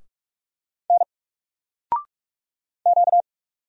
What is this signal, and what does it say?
Morse code, heard as a keyed tone, about 46 wpm, 700 Hz. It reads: N Q
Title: Single Character (Letter:Number:Punctuation:Prosign) 45wpm